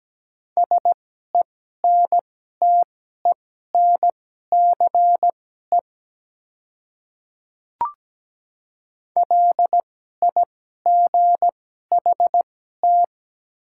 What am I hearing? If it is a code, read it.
Morse code, heard as a keyed tone, about 17 wpm, spaced as for 12 wpm, 700 Hz. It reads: SENTENCE LIGHT
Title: 2x Character Spacing - Top 200 Words - 17wpm